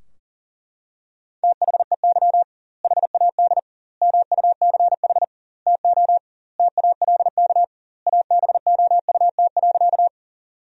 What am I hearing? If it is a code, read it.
Morse code, heard as a keyed tone, about 40 wpm, 700 Hz. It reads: THEY HAD MUCH TO TALK ABOUT.
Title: 2x Word-Spacing - Sentences from Top 400 Words - 40wpm